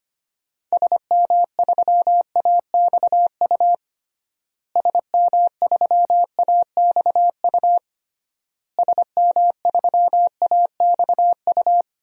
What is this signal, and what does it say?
Morse code, heard as a keyed tone, about 25 wpm, 700 Hz. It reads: SM3AXU SM3AXU SM3AXU
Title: DXCC-100 Call Signs - 6 Characters - Repeated 3x - 25wpm